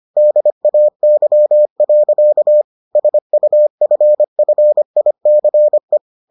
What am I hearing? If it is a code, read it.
Morse code, heard as a keyed tone, about 25 wpm, 600 Hz. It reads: DAY. SUFFICE